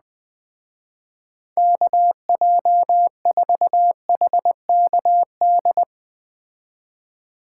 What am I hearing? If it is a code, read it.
Morse code, heard as a keyed tone, about 20 wpm, 700 Hz. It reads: KJ4HKD
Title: Speed Racing - USA 2x3 Call Signs 17wpm